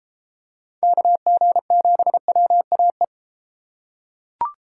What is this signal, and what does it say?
Morse code, heard as a keyed tone, about 33 wpm, 700 Hz. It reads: KG7WAE